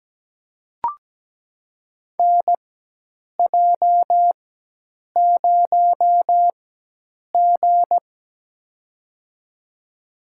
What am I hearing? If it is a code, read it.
Morse code, heard as a keyed tone, about 17 wpm, spaced as for 8 wpm, 700 Hz. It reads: NJ0G
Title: USA Call Signs - 4 Characters - 4x Character Spacing - 17wpm